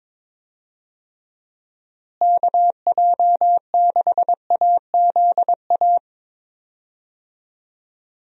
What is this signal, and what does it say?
Morse code, heard as a keyed tone, about 22 wpm, 700 Hz. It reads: KJ6AZA
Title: USA Call Signs - 6 Characters - Repeated 3x - 22wpm